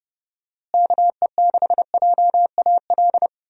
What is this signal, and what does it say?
Morse code, heard as a keyed tone, about 30 wpm, 700 Hz. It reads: KE6JAL
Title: USA Call Signs - 6 Characters - Repeated 2x - 30wpm